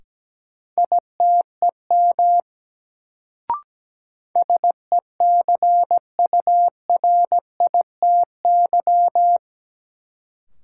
Morse code, heard as a keyed tone, about 17 wpm, 700 Hz. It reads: ITEM SECURITY
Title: Sentences from Top 1400 Words - New Word Drill Alt - 17wpm